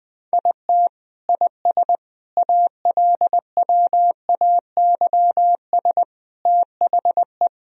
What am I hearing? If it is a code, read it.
Morse code, heard as a keyed tone, about 20 wpm, 700 Hz. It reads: IT IS ALWAYS THE
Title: Sentences from Top 1000 Words - Review Part 1 - 20wpm